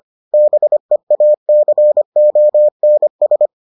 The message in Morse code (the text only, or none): BEACONS